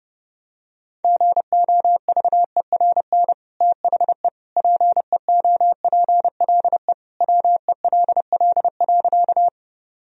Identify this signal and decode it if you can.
Morse code, heard as a keyed tone, about 31 wpm, 700 Hz. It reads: GOVERN THE PEOPLE WELL.
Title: Sentences from Top 1000 Words - Review Part 2 - 30wpm